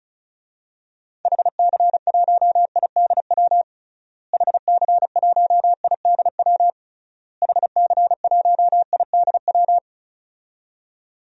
Morse code, heard as a keyed tone, about 36 wpm, 700 Hz. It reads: HC1IDW HC1IDW HC1IDW
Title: DXCC-100 Call Signs - 6 Characters - Repeated 3x - 35wpm